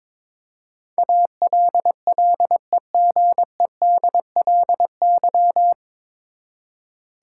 Morse code, heard as a keyed tone, about 21 wpm, 700 Hz. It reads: ALLEGEDLY
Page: Morse Code Ninja practice 22wpm